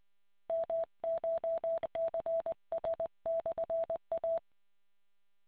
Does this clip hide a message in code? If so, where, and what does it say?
Yes: Morse code from 0.5 to 4.4 s, heard as a keyed tone, about 20 wpm, 660 Hz. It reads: MCHCS/A